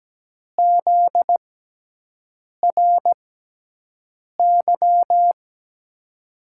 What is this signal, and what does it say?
Morse code, heard as a keyed tone, about 17 wpm, spaced as for 6 wpm, 700 Hz. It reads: ZRY
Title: USA Call Signs - 6 Characters - 6x Character Spacing - 17wpm